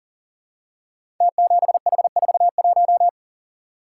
Morse code, heard as a keyed tone, about 42 wpm, 700 Hz. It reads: T7H41
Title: Code Groups - 5 Letter:Numbers - 40wpm